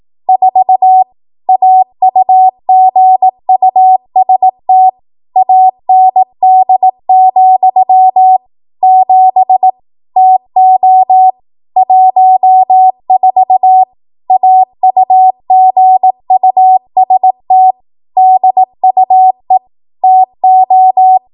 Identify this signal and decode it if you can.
Morse code, heard as a keyed tone, about 18 wpm, 750 Hz. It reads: 4 AUGUST AND, 7 TO 14 AUGUST DUE TO
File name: Morse / 18WPM